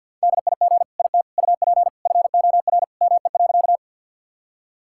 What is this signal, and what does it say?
Morse code, heard as a keyed tone, about 50 wpm, 700 Hz. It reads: DIG IT UP FOR ME.
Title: Sentences from Top 1600 Words 50wpm